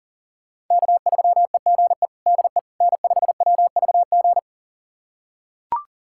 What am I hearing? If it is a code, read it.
Morse code, heard as a keyed tone, about 37 wpm, 700 Hz. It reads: K3EGE DE N5WVG
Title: USA Callsign DE Callsign - 5 Characters - 40wpm